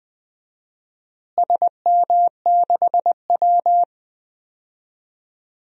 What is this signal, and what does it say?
Morse code, heard as a keyed tone, about 20 wpm, 700 Hz. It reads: SM6W
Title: DXCC-100 Call Signs - 4 Characters - Repeated 3x - 20wpm